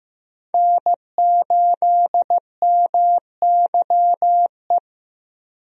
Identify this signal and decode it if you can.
Morse code, heard as a keyed tone, about 16 wpm, 700 Hz. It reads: N8MYE
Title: USA Call Signs - 5 Characters - Repeated 2x - 15wpm